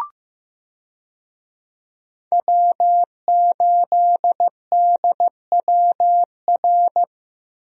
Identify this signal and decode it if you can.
Morse code, heard as a keyed tone, about 15 wpm, 700 Hz. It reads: W8DWR
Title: USA Call Signs - 5 Characters - Repeated 3x - 15wpm